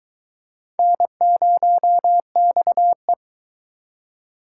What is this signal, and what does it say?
Morse code, heard as a keyed tone, about 22 wpm, 700 Hz. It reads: N0XE